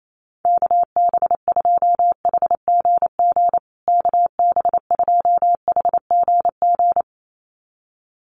Morse code, heard as a keyed tone, about 28 wpm, 700 Hz. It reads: KB2HGG KB2HGG
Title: USA Call Signs Repeated 2x 28wpm